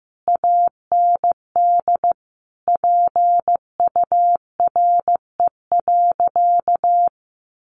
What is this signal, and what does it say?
Morse code, heard as a keyed tone, about 15 wpm, 700 Hz. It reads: AND PURE.
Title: Sentences from Top 1500 Words 15wpm